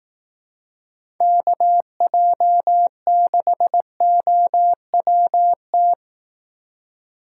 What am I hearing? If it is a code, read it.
Morse code, heard as a keyed tone, about 17 wpm, 700 Hz. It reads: KJ6OWT
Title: Speed Racing - USA 2x3 Call Signs 15wpm